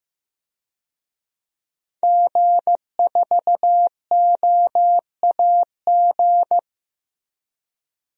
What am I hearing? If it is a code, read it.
Morse code, heard as a keyed tone, about 15 wpm, 700 Hz. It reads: G4OAG
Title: DXCC-100 Call Signs - 5 Characters - 15wpm